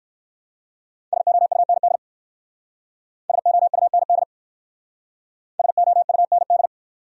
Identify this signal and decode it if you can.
Morse code, heard as a keyed tone, about 54 wpm, 700 Hz. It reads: SOUND SOUND SOUND
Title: Speed-Racing Top 300 Words 45wpm